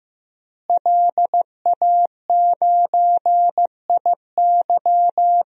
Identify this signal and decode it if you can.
Morse code, heard as a keyed tone, about 15 wpm, 700 Hz. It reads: LA9IY
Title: DXCC-100 Call Signs - 5 Characters - Repeated 2x - 15wpm